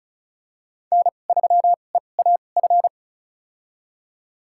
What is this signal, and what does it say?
Morse code, heard as a keyed tone, about 34 wpm, spaced as for 25 wpm, 690 Hz. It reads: N3EAF
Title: USA Call Signs - 5 Characters - 2x Character Spacing - 35wpm